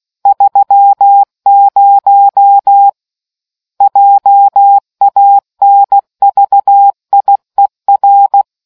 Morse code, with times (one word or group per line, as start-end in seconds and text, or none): 0.3-2.9 30
3.8-8.4 JANVIER